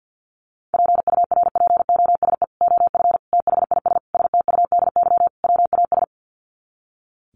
Morse code, heard as a keyed tone, about 49 wpm, 700 Hz. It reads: PURPOSE OF THIS STUDY WAS
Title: Sets of 5 Words 50wpm